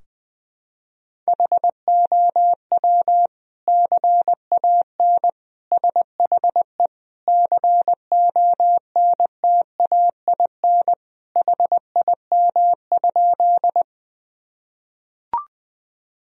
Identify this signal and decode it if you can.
Morse code, heard as a keyed tone, about 21 wpm, 700 Hz. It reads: HOW CAN SHE CONTAIN HIM?
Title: Sentences from Top 1000 Words - Review Part 2 - 20wpm